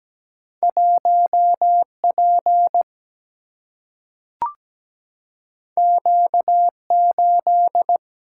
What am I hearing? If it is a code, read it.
Morse code, heard as a keyed tone, about 17 wpm, 700 Hz. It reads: 1 P Q 8
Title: Single-Letter Single-Number 17wpm